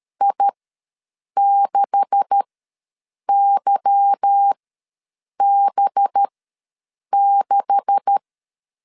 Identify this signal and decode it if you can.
Morse code, heard as a keyed tone, about 13 wpm, spaced as for 7 wpm, 800 Hz. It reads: I6YB6
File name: LEZIONI / 9 286 ws 12wpm 3351